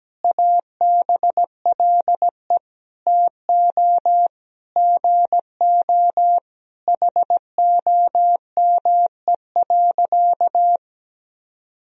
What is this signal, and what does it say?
Morse code, heard as a keyed tone, about 17 wpm, 700 Hz. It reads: ABLE TO GO HOME.
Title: Sentences from Top 1000 Words - Review Part 3 - 17wpm